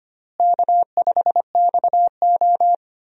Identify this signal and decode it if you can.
Morse code, heard as a keyed tone, about 25 wpm, 700 Hz. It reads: K5XO
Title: USA Call Signs - 4 Characters - Repeated 2x - 25wpm